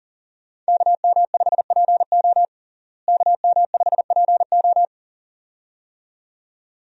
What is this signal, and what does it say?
Morse code, heard as a keyed tone, about 40 wpm, 700 Hz. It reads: KM5PO KM5PO
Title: USA Call Signs - 5 Characters - Repeated 2x - 40wpm